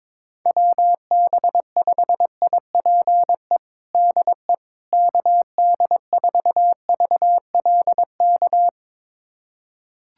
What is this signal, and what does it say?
Morse code, heard as a keyed tone, about 22 wpm, 700 Hz. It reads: WB5IPE DE KD4VLK
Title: USA Callsign DE Callsign 22wpm